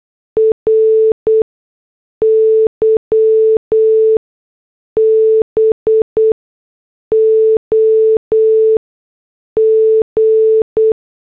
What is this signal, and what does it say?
Morse code, heard as a keyed tone, about 8 wpm, spaced as for 6 wpm, 440 Hz. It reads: RYBOG